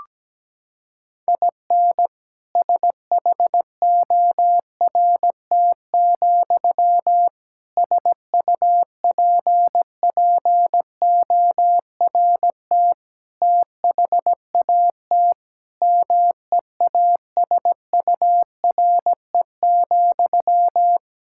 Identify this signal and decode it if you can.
Morse code, heard as a keyed tone, about 17 wpm, 700 Hz. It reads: IN SHORT, SUPPORT THAT MEASURE,
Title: Sentences from Top 1000 Words - Review Part 5 - 17wpm